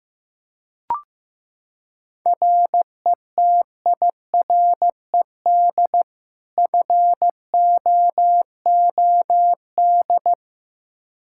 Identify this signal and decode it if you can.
Morse code, heard as a keyed tone, about 15 wpm, 700 Hz. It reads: RETIRED FOOD
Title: Practice QSO - Exchange 3 - More Info - 15wpm